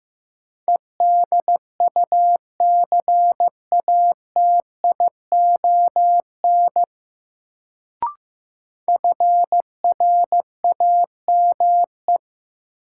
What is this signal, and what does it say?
Morse code, heard as a keyed tone, about 15 wpm, 700 Hz. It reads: EDUCATION FRAME
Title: Sentences from Top 1000 Words - Review Part 6 - New Word Drill - 15wpm